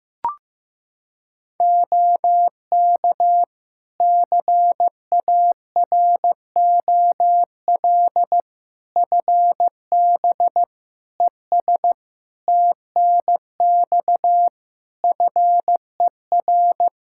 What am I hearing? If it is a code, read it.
Morse code, heard as a keyed tone, about 15 wpm, 700 Hz. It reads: OK CAROL FB ES TNX FER
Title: K1USN SST QSO Practice 15wpm